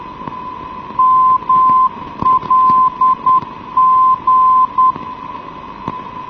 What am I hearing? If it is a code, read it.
Morse code, heard as a keyed tone, about 10 wpm, 1030 Hz. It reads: MLG
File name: europe / mlt / mlt-22-ndb mlg 1990-rvd